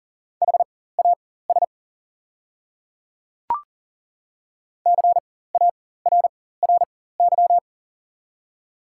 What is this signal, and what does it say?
Morse code, heard as a keyed tone, about 40 wpm, spaced as for 19 wpm, 690 Hz. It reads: HAS CARRY
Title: 4x Character Spacing - Top 300 Words - 40wpm